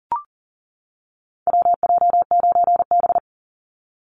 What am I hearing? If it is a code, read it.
Morse code, heard as a keyed tone, about 40 wpm, 700 Hz. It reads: WJ9B
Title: Call Signs - 40wpm